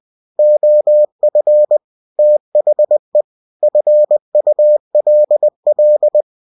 Morse code, heard as a keyed tone, about 20 wpm, 600 Hz. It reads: OF THE FULL
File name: cw-books / winnie-the-pooh / winnie-20wpm-page-0117